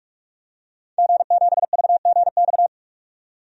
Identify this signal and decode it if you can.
Morse code, heard as a keyed tone, about 45 wpm, 700 Hz. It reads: G7VGX
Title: DXCC-100 Call Signs - 5 Characters - 45wpm